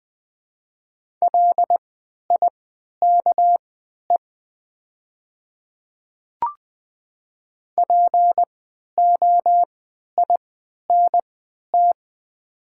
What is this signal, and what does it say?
Morse code, heard as a keyed tone, about 20 wpm, spaced as for 11 wpm, 700 Hz. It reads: LIKE POINT
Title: 3x Character Spacing - Top 200 Words - 20wpm